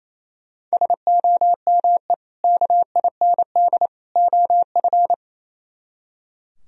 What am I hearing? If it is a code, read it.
Morse code, heard as a keyed tone, about 36 wpm, 700 Hz. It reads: SOME KIND OF